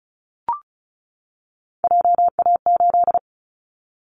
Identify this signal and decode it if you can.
Morse code, heard as a keyed tone, about 36 wpm, 700 Hz. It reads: JA8
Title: Callsign Prefixes for Top 50 DXCC - 35wpm